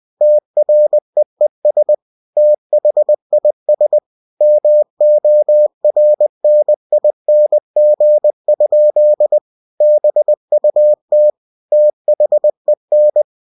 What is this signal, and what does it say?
Morse code, heard as a keyed tone, about 20 wpm, 600 Hz. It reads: TREES THIS MORNING? BUT THEN
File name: cw-books / winnie-the-pooh / winnie-20wpm-page-0041